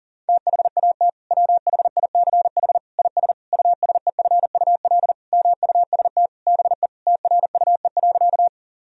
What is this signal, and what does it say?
Morse code, heard as a keyed tone, about 40 wpm, 700 Hz. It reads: THAT WHICH IS USEFUL MUST BE TRUE.
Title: Sentences from Top 1000 Words - Review Part 4 - 40wpm